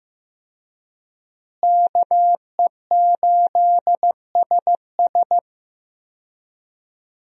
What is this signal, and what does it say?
Morse code, heard as a keyed tone, about 15 wpm, 700 Hz. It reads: KE8SS
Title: USA Call Signs - 5 Characters - Repeated 2x - 15wpm